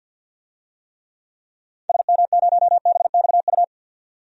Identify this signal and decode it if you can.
Morse code, heard as a keyed tone, about 48 wpm, 700 Hz. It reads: SM0BXU
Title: DXCC-100 Call Signs - 6 Characters - 50wpm